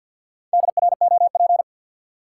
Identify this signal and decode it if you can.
Morse code, heard as a keyed tone, about 46 wpm, 700 Hz. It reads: DROP
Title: Four Letter Words - 50wpm